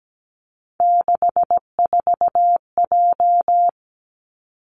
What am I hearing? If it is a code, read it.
Morse code, heard as a keyed tone, about 17 wpm, 700 Hz. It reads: 64J
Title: Three Challenging Letter-Number Combinations 17wpm